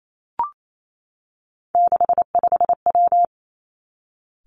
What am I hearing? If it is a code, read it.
Morse code, heard as a keyed tone, about 28 wpm, 700 Hz. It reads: 65W
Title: Three Challenging Letter-Number Combinations 28wpm